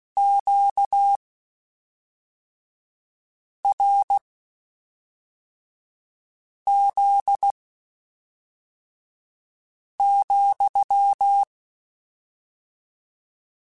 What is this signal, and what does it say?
Morse code, heard as a keyed tone, about 16 wpm, spaced as for 3 wpm, 800 Hz. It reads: QRZ,
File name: Q Calls / Q Calls-mp3s / Q Calls sppc @5